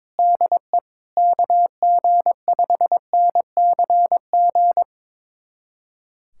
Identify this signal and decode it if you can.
Morse code, heard as a keyed tone, about 22 wpm, 700 Hz. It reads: DE KG5NCG
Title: USA Callsign DE Callsign 22wpm